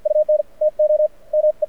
Morse code, heard as a keyed tone, about 53 wpm, spaced as for 41 wpm, 660 Hz, with radio noise. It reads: UNTO ME
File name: archived / CRSnet